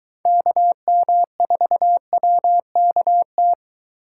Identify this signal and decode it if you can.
Morse code, heard as a keyed tone, about 23 wpm, 700 Hz. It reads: KM4WKT